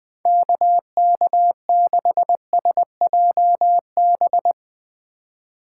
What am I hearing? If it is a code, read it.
Morse code, heard as a keyed tone, about 20 wpm, 700 Hz. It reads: KK6SJB
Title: Speed Racing - USA 2x3 Call Signs 17wpm